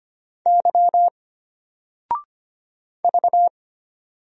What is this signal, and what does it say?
Morse code, heard as a keyed tone, about 25 wpm, 700 Hz. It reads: Y V
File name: Single-Letter-Number-25wpm